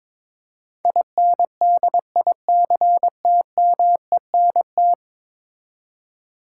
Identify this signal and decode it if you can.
Morse code, heard as a keyed tone, about 22 wpm, 700 Hz. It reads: INDICTMENT
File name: practice / 22wpm / 10 Letter Words - First 3 Letters Given - 22wpm